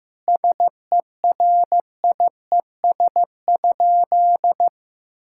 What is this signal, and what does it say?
Morse code, heard as a keyed tone, about 15 wpm, 700 Hz. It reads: SERIES?
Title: Sentences from Top 1000 Words - Review Part 5 - 15wpm